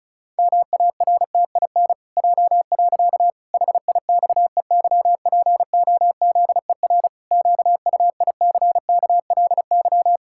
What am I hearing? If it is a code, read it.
Morse code, heard as a keyed tone, about 36 wpm, 700 Hz. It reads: MARTIN J. HIXEYPOZER QUICKLY
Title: Pangram Challenge v2 - 35wpm